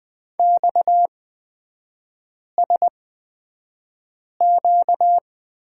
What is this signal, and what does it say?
Morse code, heard as a keyed tone, about 20 wpm, spaced as for 5 wpm, 700 Hz. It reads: XSQ
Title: 255 Lesson 41e - Period - Three Characters 20wpm